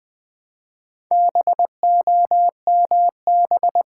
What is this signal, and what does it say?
Morse code, heard as a keyed tone, about 20 wpm, 700 Hz. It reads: BOMB
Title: Build A Word - 2 Letters At A Time - 4-Letter Words - 20wpm